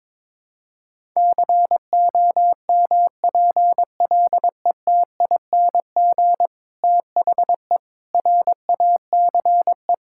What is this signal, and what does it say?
Morse code, heard as a keyed tone, about 22 wpm, 700 Hz. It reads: COMPLETING THE RACE
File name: Difficult Sentences with Hints - 22wpm